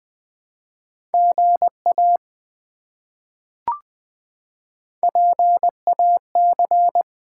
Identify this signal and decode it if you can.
Morse code, heard as a keyed tone, about 20 wpm, 700 Hz. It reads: GA PAC
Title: ARRL Sections v2 - 20wpm